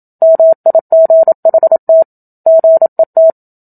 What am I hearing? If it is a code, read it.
Morse code, heard as a keyed tone, about 27 wpm, 650 Hz. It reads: MIGHT GET
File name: Book Two / 01-Chapter 1 - 27 WPM